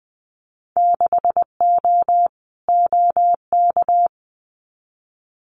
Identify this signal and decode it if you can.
Morse code, heard as a keyed tone, about 20 wpm, 700 Hz. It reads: 6O OK
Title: Winter Field Day Exchanges Encore 20wpm